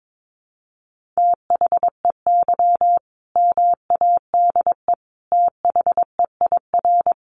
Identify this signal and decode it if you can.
Morse code, heard as a keyed tone, about 22 wpm, 700 Hz. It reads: THEY MADE THEIR
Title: Sentences from Top 1100 Words 22wpm